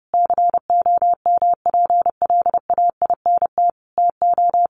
Morse code, heard as a keyed tone, about 30 wpm, 700 Hz. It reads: COMPLAINT TO
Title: Sentences from Top 1400 Words 30wpm